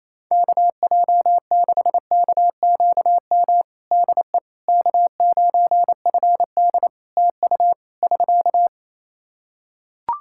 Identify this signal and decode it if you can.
Morse code, heard as a keyed tone, about 27 wpm, 700 Hz. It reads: KJ6KQM DE K9FD TU <SK>
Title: K1USN SST QSO Practice 28wpm